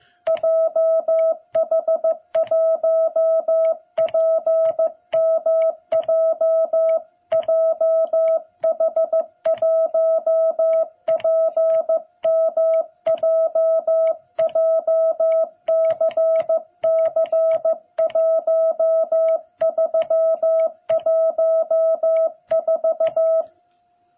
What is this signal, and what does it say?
Morse code, heard as a keyed tone, about 15 wpm, 660 Hz. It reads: JH1PMJ JH1PMJ JCC1314
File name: jh8cbh / 2021cw